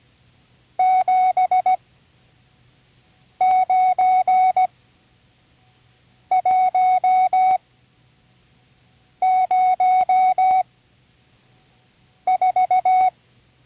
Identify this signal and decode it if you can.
Morse code, heard as a keyed tone, about 17 wpm, spaced as for 5 wpm, 740 Hz. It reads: 79104
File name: CW Learning Programs / Learning Morse Code (from Tune in the World) / 35 answers